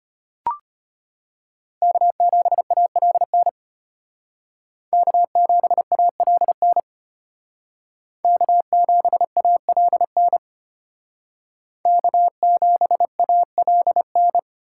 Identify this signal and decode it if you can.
Morse code, heard as a keyed tone, about 33 wpm, 700 Hz. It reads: K7ALN K7ALN K7ALN K7ALN